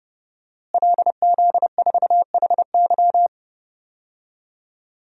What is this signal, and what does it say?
Morse code, heard as a keyed tone, about 30 wpm, 700 Hz. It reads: LZ4HY
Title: DXCC-100 Call Signs - 5 Characters - Repeated 3x - 30wpm